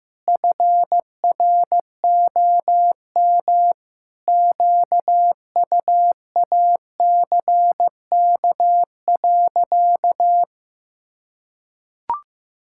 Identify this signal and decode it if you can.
Morse code, heard as a keyed tone, about 15 wpm, 700 Hz. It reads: FROM QUACK.
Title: Pangram Challenge v2 - 15wpm